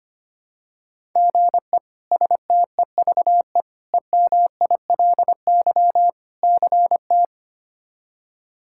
Morse code, heard as a keyed tone, about 25 wpm, 700 Hz. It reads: GE STEVE EMILY CT
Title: K1USN SST QSO Practice - 25wpm